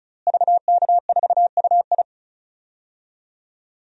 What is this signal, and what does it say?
Morse code, heard as a keyed tone, about 36 wpm, 700 Hz. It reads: VK4UI
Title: DXCC-100 Call Signs - 5 Characters - Repeated 2x - 35wpm